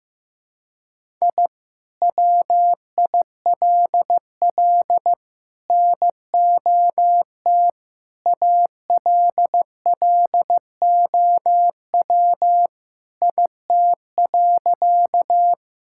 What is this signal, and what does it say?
Morse code, heard as a keyed tone, about 15 wpm, 700 Hz. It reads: I WILL NOT ALLOW IT.
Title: Sentences from Top 1000 Words - Review Part 5 - 15wpm